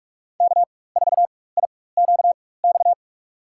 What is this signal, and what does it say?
Morse code, heard as a keyed tone, about 46 wpm, spaced as for 21 wpm, 700 Hz. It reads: K4IQX
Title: USA Call Signs - 5 Characters - 4x Character Spacing - 45wpm